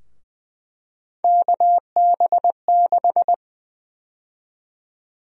Morse code, heard as a keyed tone, about 20 wpm, 700 Hz. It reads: KB6
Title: USA Call Sign Prefixes - 20wpm